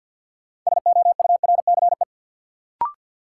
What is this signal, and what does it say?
Morse code, heard as a keyed tone, about 52 wpm, 700 Hz. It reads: SOURCE
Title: Sentences from Top 800 Words - New Word Drill - 50wpm